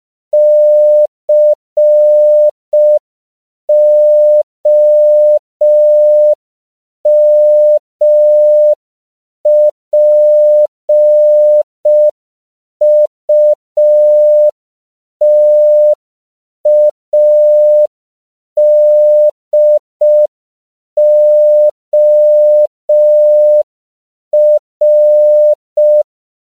Morse code, heard as a keyed tone, about 5 wpm, 600 Hz. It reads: COMPUTADOR